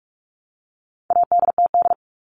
Morse code, heard as a keyed tone, about 44 wpm, 700 Hz. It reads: ADTD
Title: All Cut Numbers - 4-Digit 45wpm